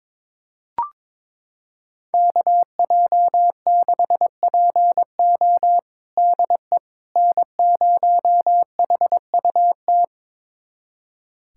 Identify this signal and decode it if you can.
Morse code, heard as a keyed tone, about 22 wpm, 700 Hz. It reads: KJ6PO DE N0HUT
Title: USA Callsign DE Callsign - 5 Characters - 22wpm